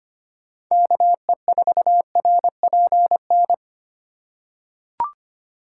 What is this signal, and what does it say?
Morse code, heard as a keyed tone, about 25 wpm, 700 Hz. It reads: KE4RPN
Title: USA Call Signs - 6 Characters - 8x Character Spacing - 25wpm